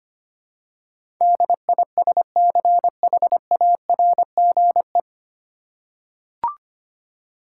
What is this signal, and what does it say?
Morse code, heard as a keyed tone, about 25 wpm, 700 Hz. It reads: DISCHARGE
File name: Nine Letter Words - 25wpm